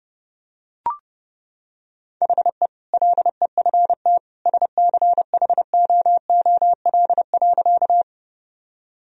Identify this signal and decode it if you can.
Morse code, heard as a keyed tone, about 32 wpm, 700 Hz. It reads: HE LEFT SCHOOL.
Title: Sentences from Top 1000 Words - Review Part 1 - 30wpm